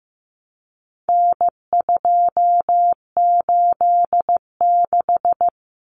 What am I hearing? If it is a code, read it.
Morse code, heard as a keyed tone, about 15 wpm, 700 Hz. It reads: N286
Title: Common Cut Numbers - 4-Digit 15wpm